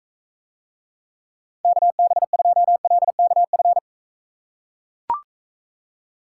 Morse code, heard as a keyed tone, about 41 wpm, 700 Hz. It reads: KB2LKF